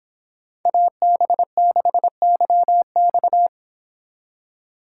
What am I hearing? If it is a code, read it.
Morse code, heard as a keyed tone, about 25 wpm, 700 Hz. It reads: AB6YX